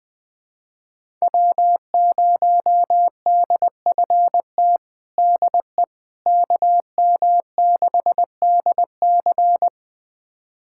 Morse code, heard as a keyed tone, about 20 wpm, 700 Hz. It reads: W0DFT DE KM6DC
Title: USA Callsign DE Callsign - 5 Characters - 20wpm